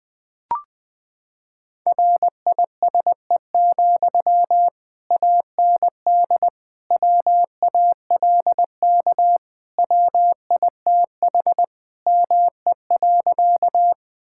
Morse code, heard as a keyed tone, about 20 wpm, 700 Hz. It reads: RISE, AND WALK WITH ME.
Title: Sentences from Top 1000 Words - Review Part 6 - 20wpm